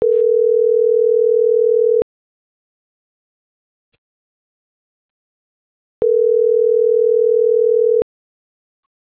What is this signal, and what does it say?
Ringback tone (North America), 440 + 480 Hz.